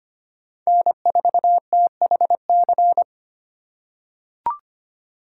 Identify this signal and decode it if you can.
Morse code, heard as a keyed tone, about 25 wpm, 700 Hz. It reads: N4THC
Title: USA Call Signs - 5 Characters - 3x Character Spacing - 25wpm